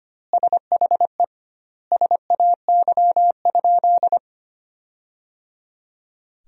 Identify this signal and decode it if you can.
Morse code, heard as a keyed tone, about 25 wpm, 700 Hz. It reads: SHE SAY?
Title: 2x Word-Spacing - Sentences from Top 200 Words - 25wpm